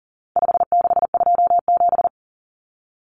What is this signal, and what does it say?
Morse code, heard as a keyed tone, about 40 wpm, 700 Hz. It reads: 5627
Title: Common Cut Numbers - 4-Digit 40wpm